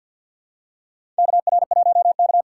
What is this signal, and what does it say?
Morse code, heard as a keyed tone, about 54 wpm, 700 Hz. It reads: KR1K
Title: USA Call Signs - 4 Characters - 6x Character Spacing - 50wpm